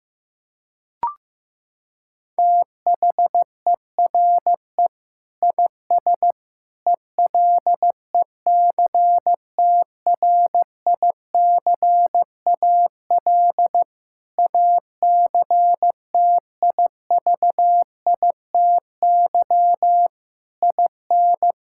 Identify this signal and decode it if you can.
Morse code, heard as a keyed tone, about 15 wpm, 700 Hz. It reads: THERE IS ELECTRICAL ACTIVITY IN
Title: Sentences from Top 1000 Words - Review Part 6 - 15wpm